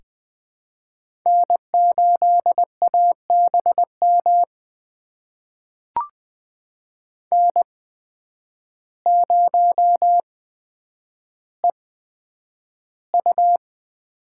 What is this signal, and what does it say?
Morse code, heard as a keyed tone, about 20 wpm, spaced as for 9 wpm, 700 Hz. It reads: N8ABM N0EU
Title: USA Call Signs - 5 Characters - 8x Character Spacing - 20wpm